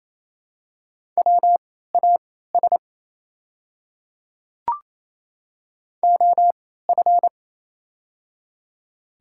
Morse code, heard as a keyed tone, about 28 wpm, spaced as for 16 wpm, 700 Hz. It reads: WAS OF
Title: 3x Character Spacing - Top 200 Words - 28wpm